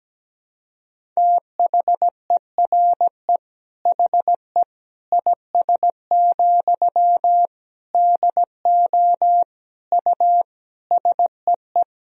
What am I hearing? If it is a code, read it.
Morse code, heard as a keyed tone, about 17 wpm, 700 Hz. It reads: THERE HE IS, DO U SEE
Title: Sentences from Top 100 Words and Abbreviations 17wpm